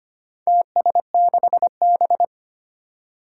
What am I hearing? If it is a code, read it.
Morse code, heard as a keyed tone, about 25 wpm, 700 Hz. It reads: TS6B
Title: Four Challenging Letter-Number Combinations 25wpm